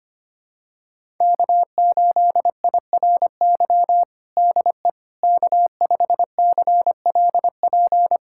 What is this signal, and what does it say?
Morse code, heard as a keyed tone, about 25 wpm, 700 Hz. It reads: K8IRY DE K5CLP
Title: USA Callsign DE Callsign - 5 Characters - 25wpm